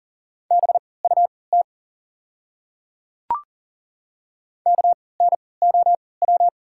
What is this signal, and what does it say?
Morse code, heard as a keyed tone, about 40 wpm, spaced as for 23 wpm, 690 Hz. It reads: BUT KNOW
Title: 3x Character Spacing - Top 300 Words - 40wpm